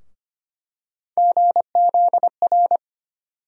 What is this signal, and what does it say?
Morse code, heard as a keyed tone, about 25 wpm, 700 Hz. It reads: GZR
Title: USA Call Sign Postfixes - 3 Letters 25wpm